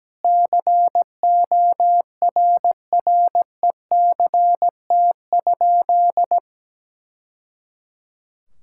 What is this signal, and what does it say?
Morse code, heard as a keyed tone, about 17 wpm, 700 Hz. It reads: CORRECT?
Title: Sentences from Top 1000 Words - Review Part 5 - 17wpm